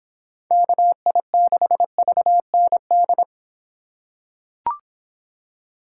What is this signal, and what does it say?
Morse code, heard as a keyed tone, about 25 wpm, 700 Hz. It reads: KI6VND